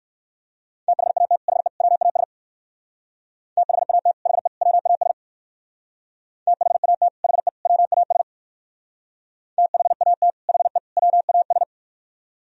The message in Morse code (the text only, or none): THAT HE WAS THAT HE WAS THAT HE WAS THAT HE WAS